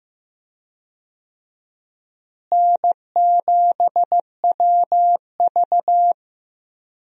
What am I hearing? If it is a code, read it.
Morse code, heard as a keyed tone, about 15 wpm, 700 Hz. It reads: N7WV
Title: USA Call Signs - 4 Characters - Repeated 2x - 15wpm